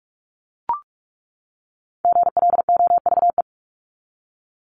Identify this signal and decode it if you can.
Morse code, heard as a keyed tone, about 44 wpm, 700 Hz. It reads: GLOVE
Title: Top 1751-2000 Words 45wpm